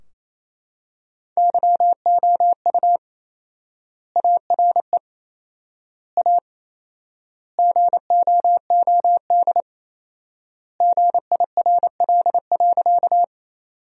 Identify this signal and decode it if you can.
Morse code, heard as a keyed tone, about 28 wpm, 700 Hz. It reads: YOU ARE A GOOD GIRL.
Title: 4x Word-Spacing - Sentences from Top 400 Words - 28wpm